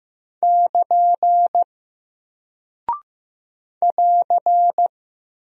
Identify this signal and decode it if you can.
Morse code, heard as a keyed tone, about 15 wpm, 700 Hz. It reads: <KN> <AR>
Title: Single Character (Letter:Number:Punctuation:Prosign) Alt 15wpm